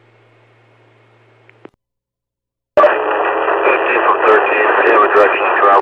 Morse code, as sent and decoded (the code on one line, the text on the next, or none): none